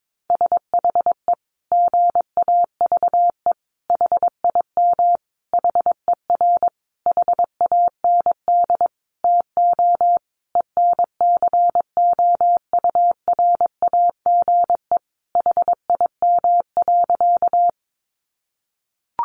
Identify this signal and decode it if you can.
Morse code, heard as a keyed tone, about 22 wpm, 700 Hz. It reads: SHE GAVE HIM HER HAND TO ENCOURAGE HIM.
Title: Sentences from Top 1400 Words 22wpm